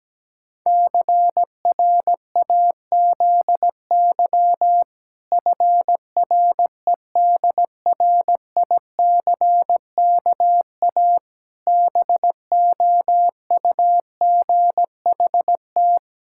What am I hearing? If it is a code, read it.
Morse code, heard as a keyed tone, about 17 wpm, 700 Hz. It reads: CRAZY FREDRICKA BOUGHT
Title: Pangram Challenge v2 - 17wpm